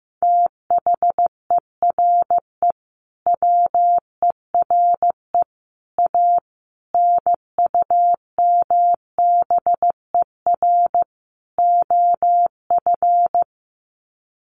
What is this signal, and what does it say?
Morse code, heard as a keyed tone, about 14 wpm, 700 Hz. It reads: THERE WERE A NUMBER OF
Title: Sets of 5 Words Encore2 15wpm